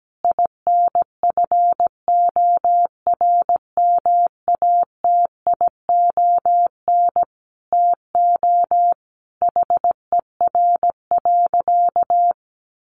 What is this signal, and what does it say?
Morse code, heard as a keyed tone, about 17 wpm, 700 Hz. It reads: INFORMATION TO HER.
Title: Sentences from Top 600 Words 17wpm